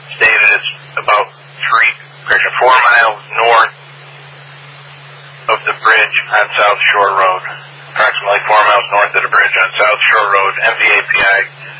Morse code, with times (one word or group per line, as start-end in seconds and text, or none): none